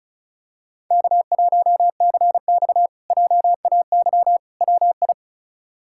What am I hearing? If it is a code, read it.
Morse code, heard as a keyed tone, about 34 wpm, 700 Hz. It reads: K1CX JAY WI
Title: K1USN SST QSO Practice - 35wpm